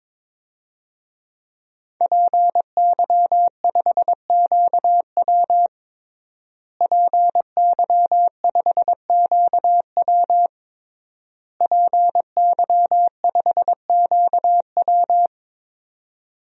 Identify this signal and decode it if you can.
Morse code, heard as a keyed tone, about 22 wpm, 700 Hz. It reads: PY5QW PY5QW PY5QW
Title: DXCC-100 Call Signs - 5 Characters - Repeated 3x - 22wpm